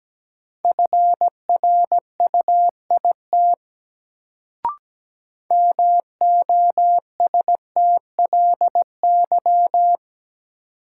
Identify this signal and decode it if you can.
Morse code, heard as a keyed tone, about 17 wpm, 700 Hz. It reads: FRUIT MOSTLY
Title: Sentences from Top 1300 Words - New Word Drill - 17wpm